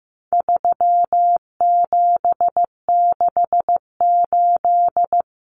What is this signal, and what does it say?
Morse code, heard as a keyed tone, about 15 wpm, 700 Hz. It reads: 3768
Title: Numbers - 4-Digit 15wpm